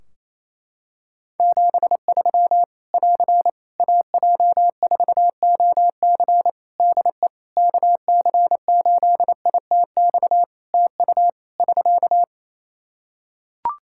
Morse code, heard as a keyed tone, about 28 wpm, 700 Hz. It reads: 73 <AR> AJ4OC DE KC8ITX TU <SK>
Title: Practice QSO - Exchange 4 - Ending - 28wpm